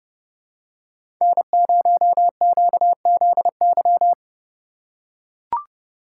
Morse code, heard as a keyed tone, about 30 wpm, 700 Hz. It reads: N0QZY
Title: USA Call Signs - 5 Characters - 3x Character Spacing - 30wpm